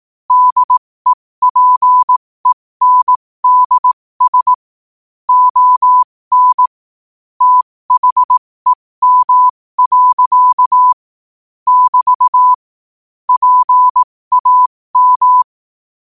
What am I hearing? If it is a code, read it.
Morse code, heard as a keyed tone, about 18 wpm, spaced as for 15 wpm, 1000 Hz. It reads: DEPENDS ON THEM. <BT> PAM
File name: qotdfiles / 15wpm